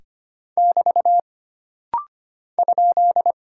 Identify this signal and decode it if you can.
Morse code, heard as a keyed tone, about 25 wpm, 700 Hz. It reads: <BT> ?
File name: Single Punctuation-Prosign - Rapid-Fire - 25wpm